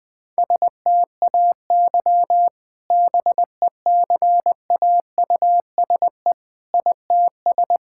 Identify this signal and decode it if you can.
Morse code, heard as a keyed tone, about 20 wpm, 700 Hz. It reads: STAY BECAUSE ITS
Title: Sentences with 2-Word Phrases - Spoken Beginning - 20wpm